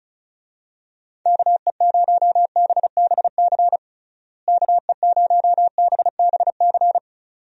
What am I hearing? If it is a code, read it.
Morse code, heard as a keyed tone, about 36 wpm, 700 Hz. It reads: KE0BBC KE0BBC
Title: USA Call Signs - 6 Characters - Repeated 2x - 35wpm